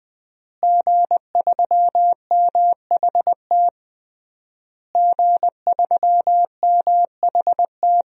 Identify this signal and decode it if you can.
Morse code, heard as a keyed tone, about 20 wpm, 700 Hz. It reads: G3MHT G3MHT
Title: DXCC-100 Call Signs - 5 Characters - Repeated 3x - 20wpm